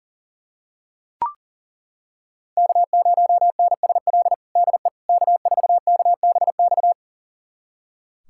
Morse code, heard as a keyed tone, about 40 wpm, 700 Hz. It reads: K0NSL DE K4KBX
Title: USA Callsign DE Callsign - 5 Characters - 40wpm